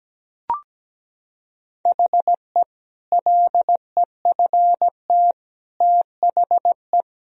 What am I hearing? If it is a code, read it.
Morse code, heard as a keyed tone, about 17 wpm, 700 Hz. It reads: HE LEFT THE
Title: Sentences from Top 1000 Words - Review Part 4 - 17wpm